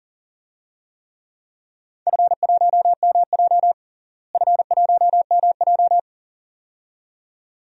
Morse code, heard as a keyed tone, about 40 wpm, 700 Hz. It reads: F1MJ F1MJ
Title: DXCC-100 Call Signs - 4 Characters - Repeated 2x - 40wpm